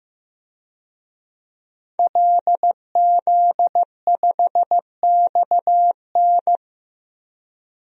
Morse code, heard as a keyed tone, about 15 wpm, 700 Hz. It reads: LZ5XN
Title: DXCC-100 Call Signs - 5 Characters - 15wpm